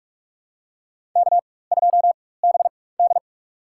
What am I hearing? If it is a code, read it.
Morse code, heard as a keyed tone, about 46 wpm, spaced as for 21 wpm, 700 Hz. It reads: K2BD
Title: USA Call Signs - 4 Characters - 4x Character Spacing - 45wpm